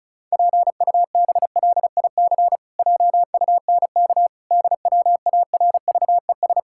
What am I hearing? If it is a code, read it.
Morse code, heard as a keyed tone, about 36 wpm, 700 Hz. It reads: PUBLIC JUNK DWARVES
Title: Pangram Challenge v2 - 35wpm